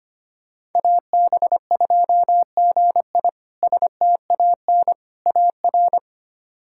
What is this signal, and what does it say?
Morse code, heard as a keyed tone, about 25 wpm, 700 Hz. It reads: AB2GI STAN AR
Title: K1USN SST QSO Practice - 25wpm